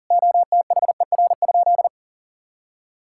Morse code, heard as a keyed tone, about 40 wpm, 700 Hz. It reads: OTHER?
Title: Sentences from Top 1600 Words 40wpm